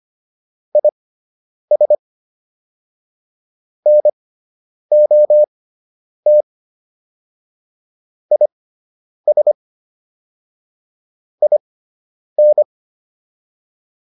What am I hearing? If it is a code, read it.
Morse code, heard as a keyed tone, about 25 wpm, spaced as for 9 wpm, 600 Hz. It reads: IS NOT IS IN